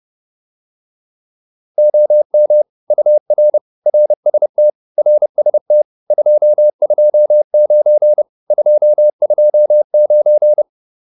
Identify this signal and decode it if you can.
Morse code, heard as a keyed tone, about 30 wpm, 600 Hz. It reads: OM UR RST RST 229 229
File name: QSO402 30